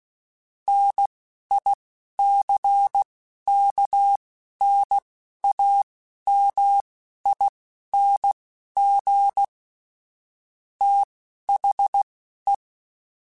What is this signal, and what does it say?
Morse code, heard as a keyed tone, about 16 wpm, spaced as for 11 wpm, 800 Hz. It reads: NICKNAMING THE
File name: Technical Subjects / Technical Subjects-mp3s / AND Gate @10